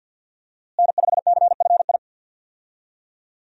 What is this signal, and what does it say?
Morse code, heard as a keyed tone, about 48 wpm, 700 Hz. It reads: N5CFI
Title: USA Call Signs - 5 Characters - Repeated 3x - 50wpm